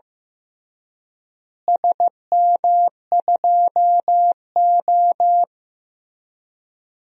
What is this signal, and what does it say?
Morse code, heard as a keyed tone, about 15 wpm, 700 Hz. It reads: SM2O
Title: DXCC-100 Call Signs - 4 Characters - Repeated 3x - 15wpm